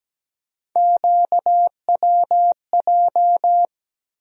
Morse code, heard as a keyed tone, about 17 wpm, 700 Hz. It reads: QWJ
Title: USA Call Sign Postfixes - 3 Letters 17wpm